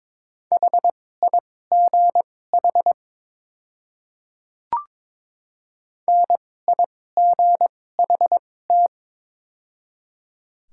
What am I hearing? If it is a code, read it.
Morse code, heard as a keyed tone, about 22 wpm, spaced as for 16 wpm, 700 Hz. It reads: HIGH NIGHT
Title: 2x Character Spacing - Top 300 Words - 22wpm